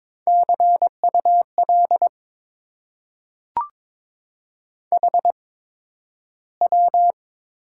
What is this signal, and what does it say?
Morse code, heard as a keyed tone, about 22 wpm, spaced as for 12 wpm, 690 Hz. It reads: CUL HW
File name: Top 90 QSO Elements - 8x Character Spacing - 22wpm